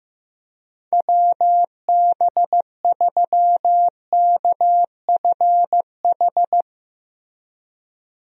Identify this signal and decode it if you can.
Morse code, heard as a keyed tone, about 15 wpm, 700 Hz. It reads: WB3KFH
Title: USA Call Signs - 6 Characters - Repeated 2x - 15wpm